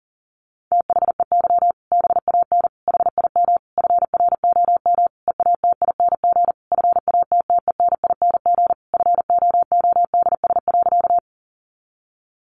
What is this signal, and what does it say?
Morse code, heard as a keyed tone, about 40 wpm, 700 Hz. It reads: THEY BAN HIM FROM EATING FATTENING FOODS.
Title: Easy Phrases 40wpm